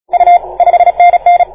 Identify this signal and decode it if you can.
Morse code, heard as a keyed tone, about 32 wpm, 690 Hz. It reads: U 5NN
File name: 1esempio-telegrafia